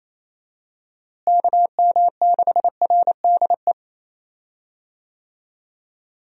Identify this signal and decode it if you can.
Morse code, heard as a keyed tone, about 28 wpm, 700 Hz. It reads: KM6RDE
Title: USA Call Signs - 6 Characters - Repeated 2x - 28wpm